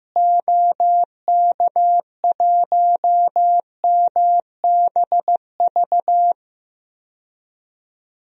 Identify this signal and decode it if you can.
Morse code, heard as a keyed tone, about 15 wpm, 700 Hz. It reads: OK1MBV
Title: DXCC-100 Call Signs - 6 Characters - Repeated 3x - 15wpm